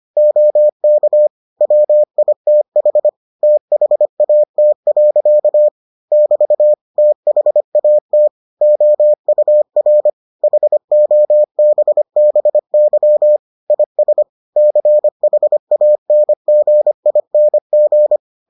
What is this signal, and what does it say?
Morse code, heard as a keyed tone, about 25 wpm, 600 Hz. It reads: OK WITH THAT. <BT> THAT OUR HOBBY IS CHANGING